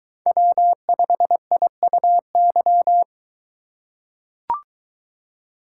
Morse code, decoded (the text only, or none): W5IUY